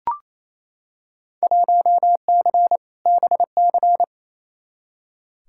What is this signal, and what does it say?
Morse code, heard as a keyed tone, about 28 wpm, 700 Hz. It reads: 1C BC
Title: Field Day Exchanges Encore - 28wpm